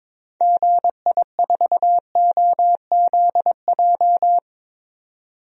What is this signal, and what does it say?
Morse code, heard as a keyed tone, about 22 wpm, 700 Hz. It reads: GI4OZJ
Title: DXCC-100 Call Signs - 6 Characters - 22wpm